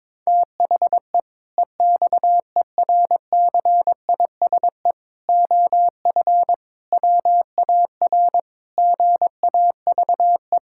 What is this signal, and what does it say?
Morse code, heard as a keyed tone, about 22 wpm, 700 Hz. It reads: THE EXERCISE OF WAR GAVE
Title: Sentences from Top 1000 Words - Review Part 6 - 22wpm